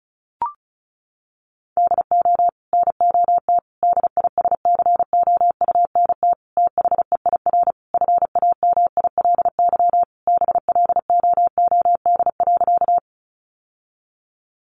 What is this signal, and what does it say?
Morse code, heard as a keyed tone, about 34 wpm, 700 Hz. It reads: DO NOT DISCOUNT THEIR FAMILY BLOOD.
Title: Sentences from Top 1200 Words 35wpm